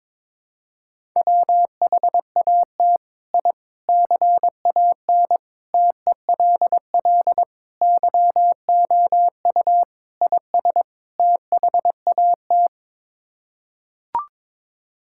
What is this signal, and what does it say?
Morse code, heard as a keyed tone, about 21 wpm, 700 Hz. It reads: WHAT I CAN TELL YOU IS THAT
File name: practice / 22wpm / Sets of 7 Words - 22wpm